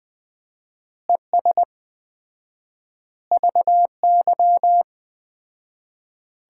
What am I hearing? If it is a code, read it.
Morse code, heard as a keyed tone, about 20 wpm, 700 Hz. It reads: ES VY
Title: Common QSO Phrases - 4x Word-Spacing - 20wpm